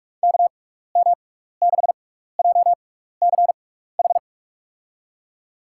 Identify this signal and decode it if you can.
Morse code, heard as a keyed tone, about 44 wpm, spaced as for 15 wpm, 700 Hz. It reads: KM6JCH
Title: USA Call Signs - 6 Characters - 6x Character Spacing - 45wpm